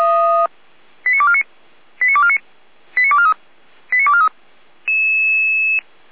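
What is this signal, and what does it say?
Five-tone selective call (ZVEI), heard as quick runs of five tones, tones of about 70 ms. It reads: 89249 89249 89243 89243
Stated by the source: Five-tone ZVEI selective call received by SDR, fire brigade Orlamünde (feuerwehr-orlamuende).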